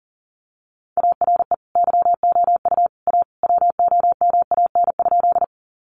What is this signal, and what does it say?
Morse code, heard as a keyed tone, about 39 wpm, 700 Hz. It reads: ARE YOU A WOMAN?
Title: Sentences from Top 1100 Words 40wpm